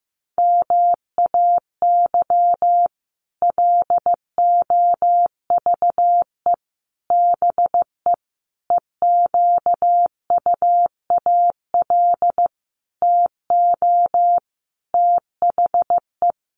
Morse code, heard as a keyed tone, about 15 wpm, 700 Hz. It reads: MAY LOVE BE EQUAL TO THE
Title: Sentences from Top 1400 Words 15wpm